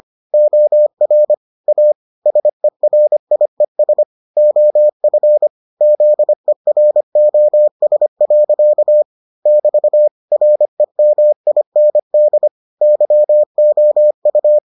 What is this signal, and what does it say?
Morse code, heard as a keyed tone, about 25 wpm, 600 Hz. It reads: OR A SERIES OF ZEROS. <BT> REMIND YOU